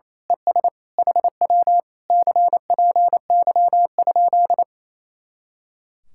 Morse code, heard as a keyed tone, about 26 wpm, 700 Hz. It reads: ES HW CPY?
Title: Speed-Racing Common QSO Phrases Encore - 28wpm